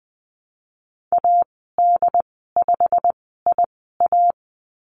Morse code, heard as a keyed tone, about 20 wpm, spaced as for 14 wpm, 700 Hz. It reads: AD5IA
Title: 088 Lesson 15g - Number 5 - Call Signs 20wpm